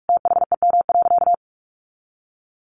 Morse code, heard as a keyed tone, about 44 wpm, 700 Hz. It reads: THEM.
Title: Sentences from Top 1500 Words 45wpm